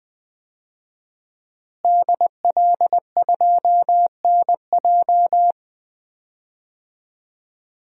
Morse code, heard as a keyed tone, about 20 wpm, 700 Hz. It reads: DL2NJ
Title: DXCC-100 Call Signs - 5 Characters - Repeated 2x - 20wpm